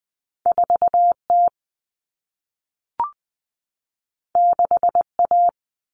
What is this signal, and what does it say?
Morse code, heard as a keyed tone, about 20 wpm, 700 Hz. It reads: 4T 6A
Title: Common Cut Numbers - 2-Digit 20wpm